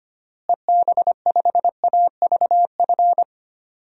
Morse code, heard as a keyed tone, about 25 wpm, 700 Hz. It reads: EB5AVF
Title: DXCC-100 Call Signs - 6 Characters - Repeated 2x - 25wpm